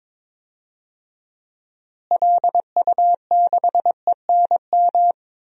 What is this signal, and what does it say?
Morse code, heard as a keyed tone, about 22 wpm, 700 Hz. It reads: LU6ENM
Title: DXCC-100 Call Signs - 6 Characters - Repeated 3x - 22wpm